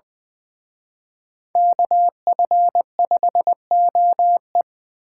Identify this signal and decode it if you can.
Morse code, heard as a keyed tone, about 24 wpm, 700 Hz. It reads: KF5OE
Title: Speed Racing - USA 2x2 Call Signs 20wpm